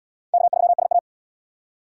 Morse code, heard as a keyed tone, about 67 wpm, 700 Hz. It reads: K3SA